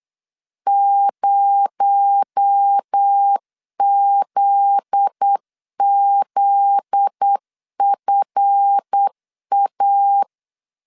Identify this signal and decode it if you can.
Morse code, heard as a keyed tone, about 8 wpm, 800 Hz. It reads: 0ZZFA
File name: LEZIONI / 7 GF9Z / 7 GF9Z 8wpm 3213